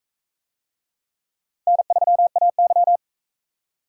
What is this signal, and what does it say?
Morse code, heard as a keyed tone, about 40 wpm, 700 Hz. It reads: N3AY